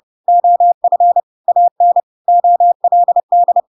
Morse code, heard as a keyed tone, about 30 wpm, 700 Hz. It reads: OF AN OLD
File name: Pinocchio / Pinocchio ch1 30 30 ik1hge